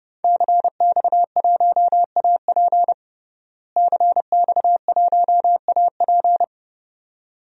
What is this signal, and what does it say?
Morse code, heard as a keyed tone, about 30 wpm, 700 Hz. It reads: CX1AP CX1AP
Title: DXCC-100 Call Signs - 5 Characters - Repeated 2x - 30wpm